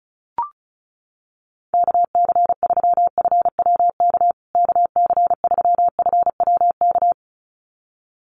Morse code, heard as a keyed tone, about 34 wpm, 700 Hz. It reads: KC3FWK KC3FWK
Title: USA Call Signs Repeated 2x 35wpm